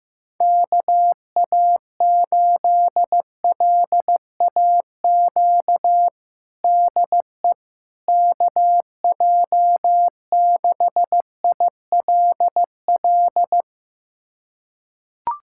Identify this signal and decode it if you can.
Morse code, heard as a keyed tone, about 15 wpm, 700 Hz. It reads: KA8LAQ DE KJ6ILL
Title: USA Callsign DE Callsign 15wpm